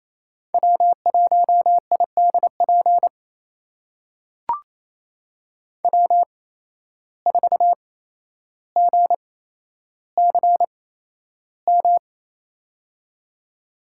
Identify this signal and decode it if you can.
Morse code, heard as a keyed tone, about 28 wpm, spaced as for 12 wpm, 700 Hz. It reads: W1IDP W4GCM
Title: USA Call Signs - 5 Characters - 8x Character Spacing - 28wpm